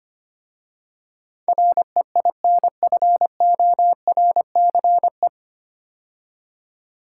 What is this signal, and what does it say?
Morse code, heard as a keyed tone, about 25 wpm, 700 Hz. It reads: REINFORCE
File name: practice / 25wpm / 9 Letter Words - First 3 Letters Given - 25wpm